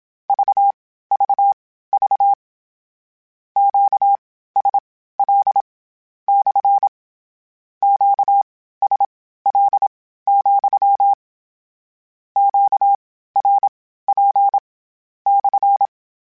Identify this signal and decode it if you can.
Morse code, heard as a keyed tone, about 27 wpm, spaced as for 15 wpm, 800 Hz. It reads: VVV QSL / QSL, QRP /
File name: Q Calls / Q Calls-mp3s / Q Calls SPSL2 @25